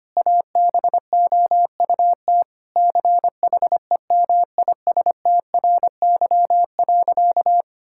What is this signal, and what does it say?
Morse code, heard as a keyed tone, about 25 wpm, 700 Hz. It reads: ABOUT CHEMISTRY.
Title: Sentences from Top 1000 Words - Review Part 6 - 25wpm